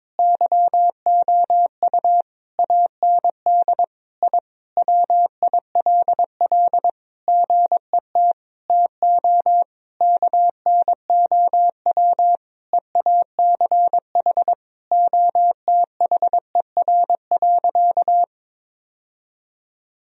Morse code, heard as a keyed tone, about 22 wpm, 700 Hz. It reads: YOU AND I WILL GET TO KNOW EACH OTHER.
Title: Sentences from Top 1000 Words - Review Part 3 - 22wpm